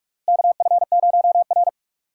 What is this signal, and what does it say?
Morse code, heard as a keyed tone, about 46 wpm, 700 Hz. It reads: KF0R
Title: USA Call Signs - 4 Characters - 3x Character Spacing - 45wpm